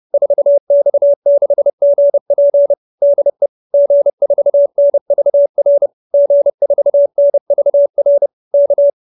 Morse code, heard as a keyed tone, about 30 wpm, 590 Hz. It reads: 4X6GP DE G4NVR G4NVR K